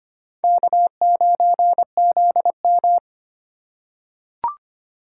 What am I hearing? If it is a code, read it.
Morse code, heard as a keyed tone, about 25 wpm, 700 Hz. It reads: K9ZM
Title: USA Call Signs - 4 Characters - 3x Character Spacing - 25wpm